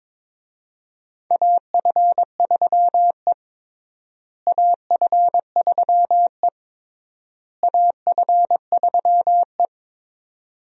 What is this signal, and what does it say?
Morse code, heard as a keyed tone, about 22 wpm, 700 Hz. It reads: AF3E AF3E AF3E
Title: USA Call Signs - 4 Characters - Repeated 3x - 22wpm